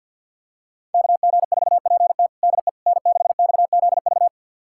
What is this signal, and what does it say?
Morse code, heard as a keyed tone, about 53 wpm, 700 Hz. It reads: KG4PT DE N6XZV
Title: USA Callsign DE Callsign - 5 Characters - 50wpm